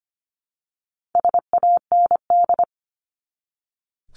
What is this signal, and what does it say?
Morse code, heard as a keyed tone, about 25 wpm, 700 Hz. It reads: SAND
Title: Top 1501-1750 Words 25wpm